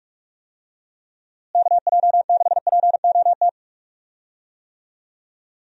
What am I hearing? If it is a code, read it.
Morse code, heard as a keyed tone, about 44 wpm, 700 Hz. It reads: KJ6POT
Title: USA Call Signs - 6 Characters - Repeated 2x - 45wpm